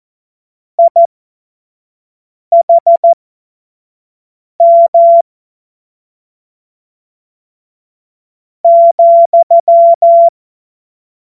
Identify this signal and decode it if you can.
Morse code, heard as a keyed tone, about 14 wpm, spaced as for 4 wpm, 680 Hz. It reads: IHM,
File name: Morsekurs / Witze / Witz 04. April 2023 25 F70